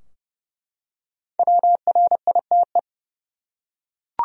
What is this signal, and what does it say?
Morse code, heard as a keyed tone, about 32 wpm, 700 Hz. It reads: WRITE
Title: Top 100 Words 30wpm